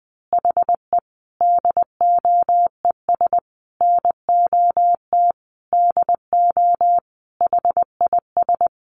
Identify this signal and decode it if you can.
Morse code, heard as a keyed tone, about 20 wpm, 700 Hz. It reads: HE DOES NOT DO HIS
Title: Additional Easy Phrases 20wpm